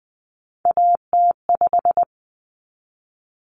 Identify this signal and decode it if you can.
Morse code, heard as a keyed tone, about 20 wpm, 700 Hz. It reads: AT5
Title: Common Cut Numbers - 3-Digit 20wpm